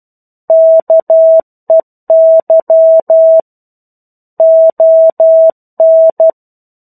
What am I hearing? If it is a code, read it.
Morse code, heard as a keyed tone, about 12 wpm, 650 Hz. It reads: KEY ON
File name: 01-Chapter-01 12 WPM